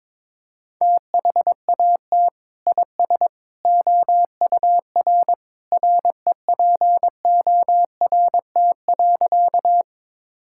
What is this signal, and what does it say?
Morse code, heard as a keyed tone, about 21 wpm, 700 Hz. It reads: THAT IS OUR REPORT.
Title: Sentences from Top 1000 Words - Review Part 6 - 22wpm